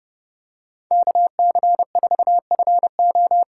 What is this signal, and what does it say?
Morse code, heard as a keyed tone, about 29 wpm, 700 Hz. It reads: KC4FO
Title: USA Call Signs - 5 Characters - 2x Character Spacing - 30wpm